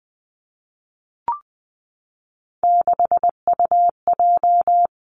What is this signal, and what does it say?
Morse code, heard as a keyed tone, about 20 wpm, 700 Hz. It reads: 6UJ
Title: Three Challenging Letter-Number Combinations 20wpm